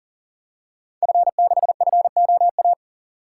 Morse code, heard as a keyed tone, about 38 wpm, 690 Hz. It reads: F6FOA
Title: Code Groups - 5 Letter:Numbers - 40wpm